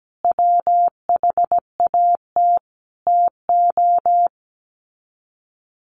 Morse code, heard as a keyed tone, about 17 wpm, 700 Hz. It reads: WHAT TO
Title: Sets of 5 Words 17wpm